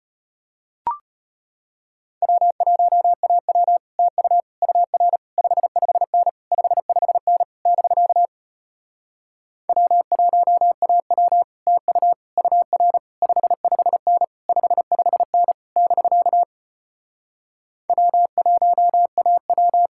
Morse code, heard as a keyed tone, about 33 wpm, 700 Hz. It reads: W1AW TU UR 55N 55N <BK> W1AW TU UR 55N 55N <BK> W1AW